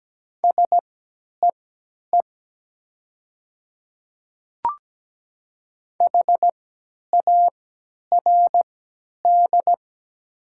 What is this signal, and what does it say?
Morse code, heard as a keyed tone, about 17 wpm, spaced as for 10 wpm, 690 Hz. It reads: SEE HARD
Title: 3x Character Spacing - Top 300 Words - 17wpm